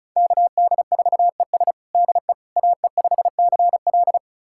Morse code, heard as a keyed tone, about 36 wpm, 700 Hz. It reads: KD4ES DE AE5CL
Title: USA Callsign DE Callsign - 5 Characters - 35wpm